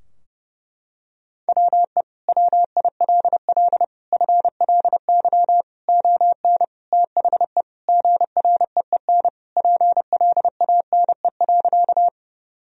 Morse code, heard as a keyed tone, about 30 wpm, 700 Hz. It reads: WE WILL FLY ON THE GREEN PLANE.
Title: Sentences from Top 1000 Words - Review Part 3 - 30wpm